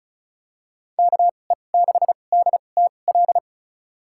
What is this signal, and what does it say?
Morse code, heard as a keyed tone, about 34 wpm, spaced as for 25 wpm, 700 Hz. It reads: KE6DTL
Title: USA Call Signs - 6 Characters - 2x Character Spacing - 35wpm